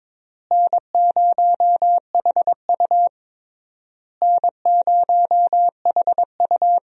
Morse code, heard as a keyed tone, about 22 wpm, 700 Hz. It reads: N0HU N0HU
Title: USA Call Signs - 4 Characters - Repeated 3x - 22wpm